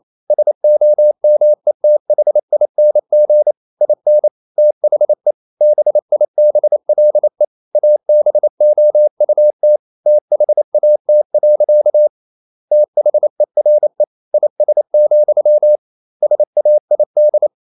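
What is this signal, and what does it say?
Morse code, heard as a keyed tone, about 28 wpm, 600 Hz. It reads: SOMETHING IN THE BIBLE ABOUT THAT. THERE IS, SAID
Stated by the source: Morse code practice at 28 wpm, text of Ken Follett's novel Never in English, from CW med Gnister.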